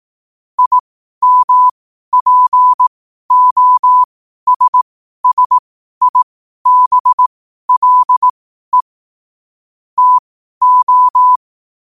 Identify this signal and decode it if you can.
Morse code, heard as a keyed tone, about 18 wpm, spaced as for 12 wpm, 1000 Hz. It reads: IMPOSSIBLE TO
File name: qotdfiles / 12wpm